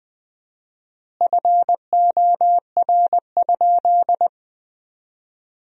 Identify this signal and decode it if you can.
Morse code, heard as a keyed tone, about 20 wpm, 700 Hz. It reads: FOR?
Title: 4x Word-Spacing - Sentences from Top 400 Words - 20wpm